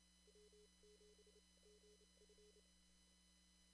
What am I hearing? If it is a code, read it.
Morse code, heard as a keyed tone, about 29 wpm, spaced as for 23 wpm, 420 Hz. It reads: W7GF
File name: CW / kt5j wpx cw-op-k5pi